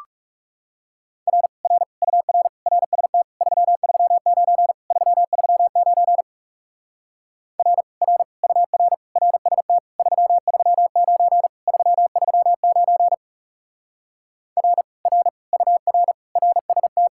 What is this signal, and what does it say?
Morse code, heard as a keyed tone, about 38 wpm, 700 Hz. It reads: R R UR RST 339 339 R R UR RST 339 339 R R UR RST